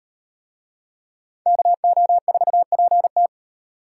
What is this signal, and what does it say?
Morse code, heard as a keyed tone, about 38 wpm, 700 Hz. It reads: KO4PT